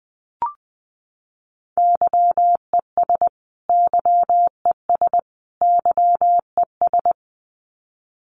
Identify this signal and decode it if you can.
Morse code, heard as a keyed tone, about 20 wpm, 700 Hz. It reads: YES YES YES
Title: Top 500 Words Repeated 20wpm